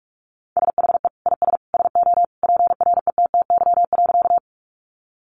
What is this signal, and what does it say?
Morse code, heard as a keyed tone, about 44 wpm, 700 Hz. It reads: SHE IS SO PRETTY.
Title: Sentences from Top 1000 Words 45wpm